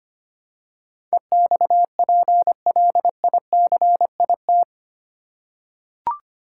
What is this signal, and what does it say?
Morse code, heard as a keyed tone, about 25 wpm, 700 Hz. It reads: EXPLICIT
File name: Eight Letter Words - 25wpm